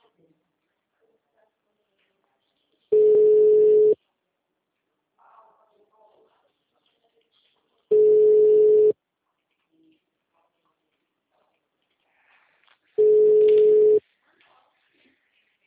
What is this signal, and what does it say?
Ringback tone (Europe, 425 Hz), with speech.